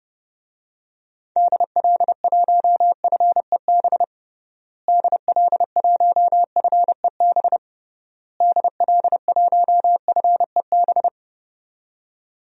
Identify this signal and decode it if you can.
Morse code, heard as a keyed tone, about 32 wpm, 700 Hz. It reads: DL1FEB DL1FEB DL1FEB
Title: DXCC-100 Call Signs - 6 Characters - Repeated 3x - 30wpm